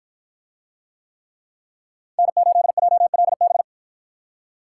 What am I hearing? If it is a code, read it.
Morse code, heard as a keyed tone, about 49 wpm, 700 Hz. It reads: N8JLB